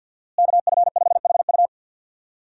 Morse code, heard as a keyed tone, about 52 wpm, 700 Hz. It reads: KF5HU
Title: USA Call Signs - 5 Characters - Repeated 2x - 50wpm